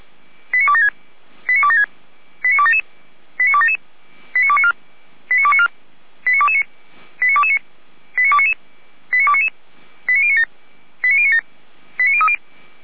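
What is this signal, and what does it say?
Five-tone selective call (ZVEI), heard as quick runs of five tones, tones of about 70 ms. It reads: 89276 89276 89277 89277 89294 89294 89229 89229 89299 89299 89097 89097 89030
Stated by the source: ZVEI five-tone selective call received by SDR.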